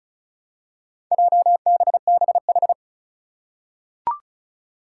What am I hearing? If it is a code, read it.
Morse code, heard as a keyed tone, about 36 wpm, 700 Hz. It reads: JBBH
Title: Four Challenging Letter-Number Combinations 35wpm